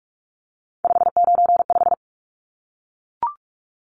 Morse code, heard as a keyed tone, about 44 wpm, 700 Hz. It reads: 595
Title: Numbers - 3-Digit 45wpm